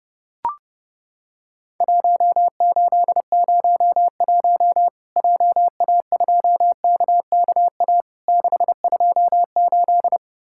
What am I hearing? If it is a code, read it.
Morse code, heard as a keyed tone, about 30 wpm, 700 Hz. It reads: 1801 JA2KKA 628
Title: Worked All Europe DX Contest - QTC Exchange - 30wpm